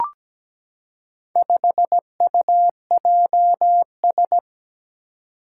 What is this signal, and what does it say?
Morse code, heard as a keyed tone, about 17 wpm, 700 Hz. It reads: 5UJS
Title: Four Challenging Letter-Number Combinations 17wpm